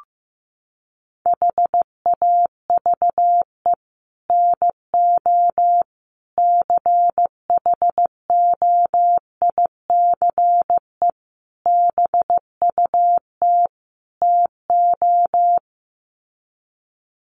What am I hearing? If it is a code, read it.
Morse code, heard as a keyed tone, about 15 wpm, 700 Hz. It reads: HAVE NO CHOICE BUT TO
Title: Sets of 5 Words Encore 15wpm